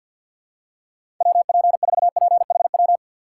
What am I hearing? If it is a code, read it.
Morse code, heard as a keyed tone, about 51 wpm, 700 Hz. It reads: WP4PHW
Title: USA Call Signs - 4x Character Spacing - 50wpm